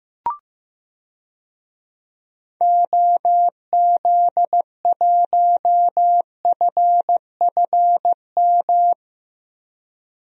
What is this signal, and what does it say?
Morse code, heard as a keyed tone, about 15 wpm, 700 Hz. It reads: OZ1FFM
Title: DXCC-100 Call Signs - 6 Characters - 15wpm